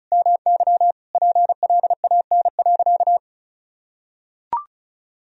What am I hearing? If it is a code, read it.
Morse code, heard as a keyed tone, about 36 wpm, 700 Hz. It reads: MY PLAN.
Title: Sentences from Top 500 Words and Abbreviations 35wpm